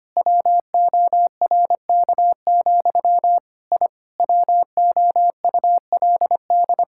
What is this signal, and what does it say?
Morse code, heard as a keyed tone, about 25 wpm, 700 Hz. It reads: WORK, I WOULD
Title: Sentences from Top 1000 Words - Review Part 6 - 25wpm